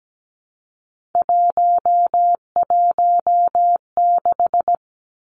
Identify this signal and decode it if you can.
Morse code, heard as a keyed tone, about 17 wpm, 700 Hz. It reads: 116
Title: Numbers - 3-Digit 17wpm